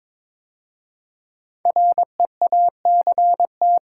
Morse code, heard as a keyed tone, about 22 wpm, 700 Hz. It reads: REACT
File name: 5 Letter Words - First 3 Letters Given - 22wpm